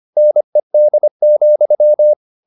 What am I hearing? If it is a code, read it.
Morse code, heard as a keyed tone, about 25 wpm, 600 Hz. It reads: NED,
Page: Ferie 25 wpm